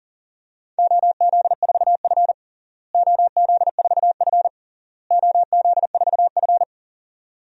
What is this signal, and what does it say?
Morse code, heard as a keyed tone, about 40 wpm, 700 Hz. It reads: OZ4F OZ4F OZ4F
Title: DXCC-100 Call Signs - 4 Characters - Repeated 3x - 40wpm